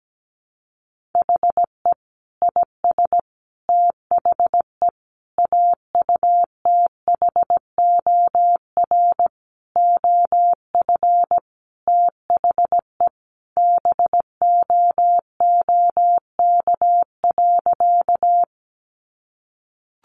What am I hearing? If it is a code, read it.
Morse code, heard as a keyed tone, about 17 wpm, 700 Hz. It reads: HE IS THE AUTHOR OF THE BOOK.
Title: Sentences from Top 1200 Words 17wpm